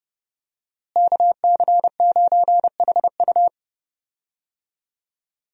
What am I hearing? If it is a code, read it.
Morse code, heard as a keyed tone, about 30 wpm, 700 Hz. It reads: KC9HU
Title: USA Call Signs - 5 Characters - Repeated 3x - 30wpm